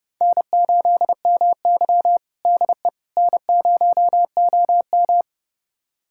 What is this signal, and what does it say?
Morse code, heard as a keyed tone, about 31 wpm, 700 Hz. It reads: N8MY DE N0OM
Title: USA Callsign DE Callsign - 4 Characters - 30wpm